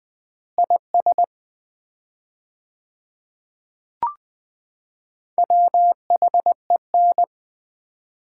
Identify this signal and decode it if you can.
Morse code, heard as a keyed tone, about 20 wpm, 700 Hz. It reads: IS WHEN
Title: 4x Word-Spacing - Sets of 2 Words - 20wpm